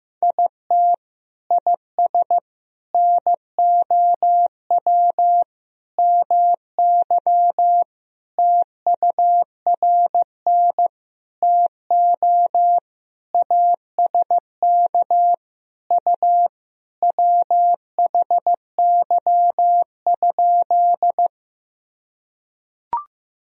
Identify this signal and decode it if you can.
Morse code, heard as a keyed tone, about 15 wpm, 700 Hz. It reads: IT IS NOW MY TURN TO ASK U WHY?
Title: Sentences from Top 200 Words and Abbreviations 15wpm